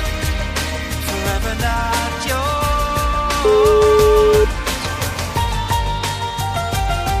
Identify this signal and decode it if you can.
Ringback tone (Europe, 425 Hz), with noise.